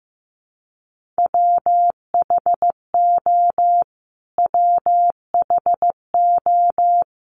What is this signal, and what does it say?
Morse code, heard as a keyed tone, about 15 wpm, 700 Hz. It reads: WHO WHO
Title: Top 500 Words Repeated 15wpm